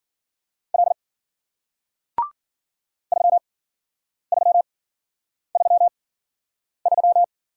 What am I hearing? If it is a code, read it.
Morse code, heard as a keyed tone, about 52 wpm, spaced as for 9 wpm, 700 Hz. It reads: L 3333